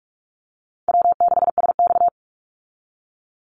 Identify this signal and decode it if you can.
Morse code, heard as a keyed tone, about 44 wpm, 700 Hz. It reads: W6SX
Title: Call Signs Repeated 3x - 45wpm